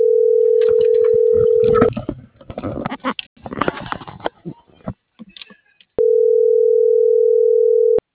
Ringback tone (North America), 440 + 480 Hz.